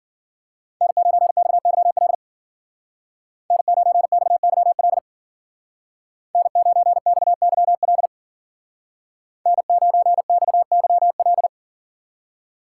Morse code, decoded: N9XYL N9XYL N9XYL N9XYL

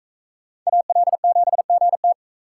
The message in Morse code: AL7GT

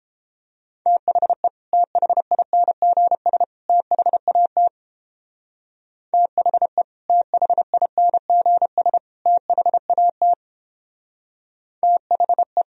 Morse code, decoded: THE THINGS THAT THE THINGS THAT THE

